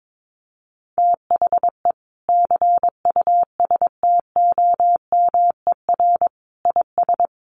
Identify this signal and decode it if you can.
Morse code, heard as a keyed tone, about 22 wpm, 700 Hz. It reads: THE CUSTOMER IS
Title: Sentences from Top 1100 Words 22wpm